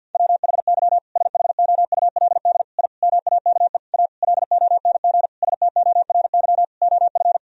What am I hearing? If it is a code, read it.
Morse code, heard as a keyed tone, about 53 wpm, 700 Hz. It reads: WHY SHOULD I MAKE A LONG STORY OF
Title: Sentences from Top 1000 Words - Review Part 1 - 50wpm